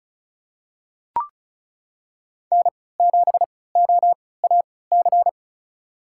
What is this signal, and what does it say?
Morse code, heard as a keyed tone, about 34 wpm, spaced as for 20 wpm, 700 Hz. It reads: N7OAC
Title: USA Call Signs - 5 Characters - 3x Character Spacing - 35wpm